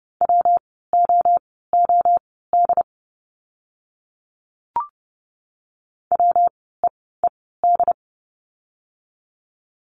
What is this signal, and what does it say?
Morse code, heard as a keyed tone, about 30 wpm, spaced as for 17 wpm, 700 Hz. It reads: WOOD WEED
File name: course / 30:10wpm / 115 Lesson 19f - Letter w - Words 30wpm